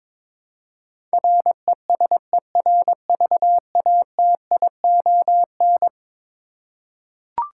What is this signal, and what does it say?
Morse code, heard as a keyed tone, about 22 wpm, 700 Hz. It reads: RESERVATION
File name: practice / 22wpm / Eleven Letter Words - 22wpm